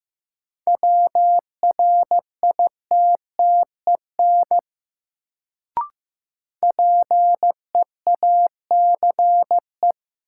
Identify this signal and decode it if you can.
Morse code, heard as a keyed tone, about 15 wpm, 700 Hz. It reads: WRITTEN PEACE
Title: Sentences from Top 1500 Words - New Word Drill Alt - 15wpm